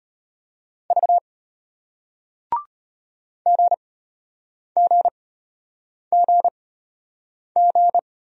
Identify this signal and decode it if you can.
Morse code, heard as a keyed tone, about 33 wpm, spaced as for 8 wpm, 700 Hz. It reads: V GGGG